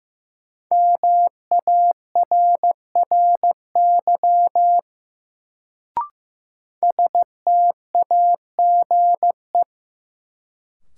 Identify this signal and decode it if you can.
Morse code, heard as a keyed tone, about 15 wpm, 700 Hz. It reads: MARRY STAGE
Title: Sentences from Top 1200 Words - New Word Drill - 15wpm